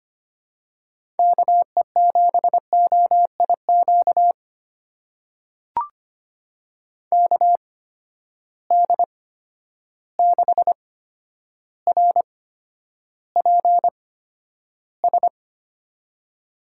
Morse code, heard as a keyed tone, about 25 wpm, spaced as for 11 wpm, 700 Hz. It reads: KE7OIQ KD6RPS
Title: USA Call Signs - 6 Characters - 8x Character Spacing - 25wpm